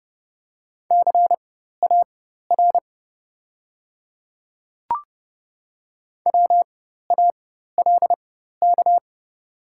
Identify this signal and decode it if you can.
Morse code, heard as a keyed tone, about 30 wpm, spaced as for 14 wpm, 700 Hz. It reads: CAR WALK
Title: 4x Character Spacing - Top 300 Words - 30wpm